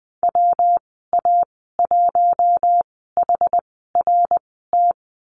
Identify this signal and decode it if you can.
Morse code, heard as a keyed tone, about 21 wpm, spaced as for 15 wpm, 700 Hz. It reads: WA1HRT
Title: 122 Lesson 20g - Number 3 - Call Signs 20wpm